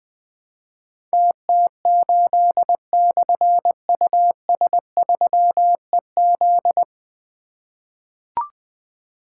Morse code, heard as a keyed tone, about 20 wpm, 700 Hz. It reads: TT8/US3EZ
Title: Difficult Call Signs 20wpm